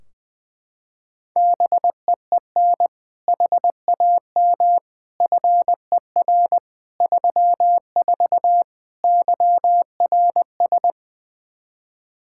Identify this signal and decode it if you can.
Morse code, heard as a keyed tone, about 20 wpm, 700 Hz. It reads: BEEN HAM FER 34 YRS
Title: Practice QSO - Exchange 3 - More Info - 20wpm